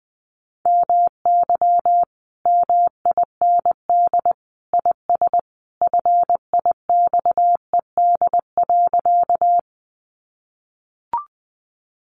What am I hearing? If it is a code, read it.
Morse code, heard as a keyed tone, about 20 wpm, 700 Hz. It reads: MY MIND IS FIXED.
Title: Sentences from Top 1100 Words 20wpm